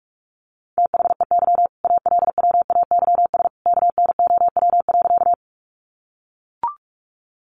Morse code, heard as a keyed tone, about 44 wpm, 700 Hz. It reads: THEY ALWAYS KNOW.
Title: Sentences from Top 300 Words 45wpm